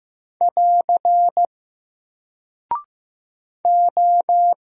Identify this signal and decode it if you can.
Morse code, heard as a keyed tone, about 15 wpm, 700 Hz. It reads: <AR> O
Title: Single Character (Letter:Number:Punctuation:Prosign) Alt 15wpm